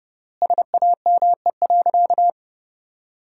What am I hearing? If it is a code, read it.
Morse code, heard as a keyed tone, about 30 wpm, 700 Hz. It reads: SAME.
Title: 3x Word-Spacing - Sentences from Top 300 Words - 30wpm